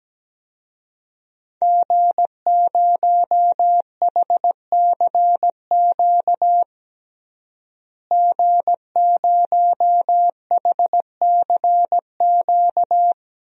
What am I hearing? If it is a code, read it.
Morse code, heard as a keyed tone, about 17 wpm, 700 Hz. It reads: G0HCQ G0HCQ
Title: DXCC-100 Call Signs - 5 Characters - Repeated 3x - 17wpm